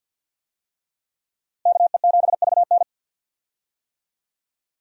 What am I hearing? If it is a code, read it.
Morse code, heard as a keyed tone, about 49 wpm, 700 Hz. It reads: KE7VN
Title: USA Call Signs - 5 Characters - Repeated 2x - 50wpm